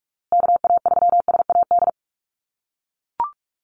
Morse code, heard as a keyed tone, about 44 wpm, 690 Hz. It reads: KA3SAD
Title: USA Call Signs Repeated 2x 45wpm